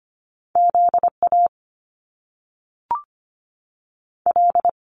Morse code, heard as a keyed tone, about 25 wpm, 700 Hz. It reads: Z A L
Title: Top Two Letter Combinations in Words 25wpm